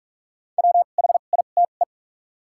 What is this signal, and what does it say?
Morse code, heard as a keyed tone, about 45 wpm, spaced as for 32 wpm, 700 Hz. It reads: WHITE
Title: 2x Character Spacing - Top 500 Words - 45wpm